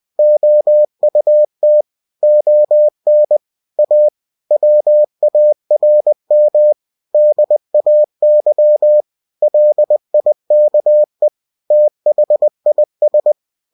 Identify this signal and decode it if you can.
Morse code, heard as a keyed tone, about 20 wpm, 600 Hz. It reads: OUT ON A WARM DAY LIKE THIS